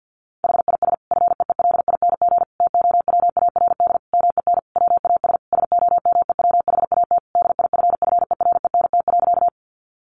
Morse code, heard as a keyed tone, about 48 wpm, 700 Hz. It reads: HIS FEELING TOWARD MEN WAS SOMEWHAT DIFFERENT.
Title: Sentences from Top 1300 Words 50wpm